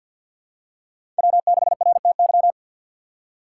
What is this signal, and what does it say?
Morse code, heard as a keyed tone, about 52 wpm, 700 Hz. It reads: W6RTY
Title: USA Call Signs - 5 Characters - 4x Character Spacing - 50wpm